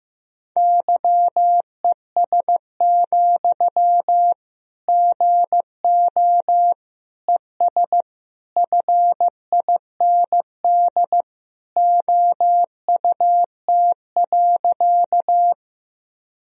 Morse code, heard as a keyed tone, about 15 wpm, 700 Hz. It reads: YES, GO ES FIND OUT.
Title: Sentences from Top 500 Words and Abbreviations 15wpm